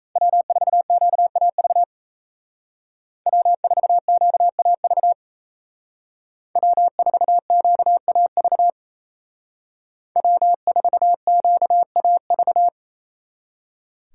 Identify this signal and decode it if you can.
Morse code, heard as a keyed tone, about 37 wpm, 700 Hz. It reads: W4QAV W4QAV W4QAV W4QAV